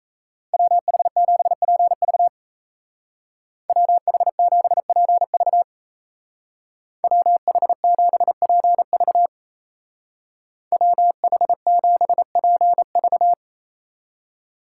Morse code, heard as a keyed tone, about 33 wpm, 700 Hz. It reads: WH7PV WH7PV WH7PV WH7PV
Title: Speed Racing - USA 2x2 Call Signs 28wpm